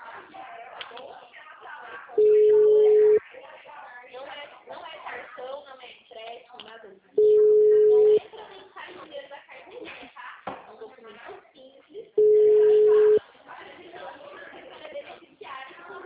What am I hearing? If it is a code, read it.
Ringback tone (Europe, 425 Hz), with speech.
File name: out- -2003- - - .54666